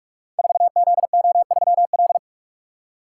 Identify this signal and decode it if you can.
Morse code, heard as a keyed tone, about 44 wpm, 690 Hz. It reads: 4ZO3L
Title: Code Groups - 5 Letter:Numbers - 45wpm